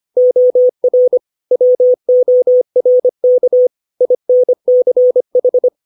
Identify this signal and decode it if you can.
Morse code, heard as a keyed tone, about 25 wpm, 500 Hz. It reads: OR WORK INCH
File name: 632 words / 632 Words 25WPM